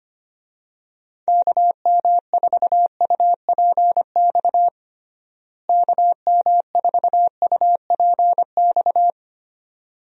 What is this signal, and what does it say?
Morse code, heard as a keyed tone, about 25 wpm, 700 Hz. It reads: KM4UPX KM4UPX
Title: USA Call Signs - 6 Characters - Repeated 2x - 25wpm